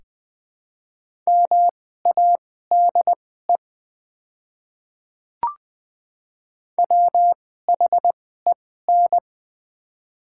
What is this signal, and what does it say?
Morse code, heard as a keyed tone, about 21 wpm, spaced as for 15 wpm, 700 Hz. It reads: MADE WHEN
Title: 2x Character Spacing - Top 200 Words - 20wpm